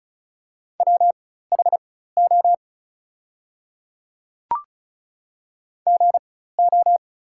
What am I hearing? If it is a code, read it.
Morse code, heard as a keyed tone, about 34 wpm, spaced as for 16 wpm, 700 Hz. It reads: WHO GO
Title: 4x Character Spacing - Top 100 Words - 35wpm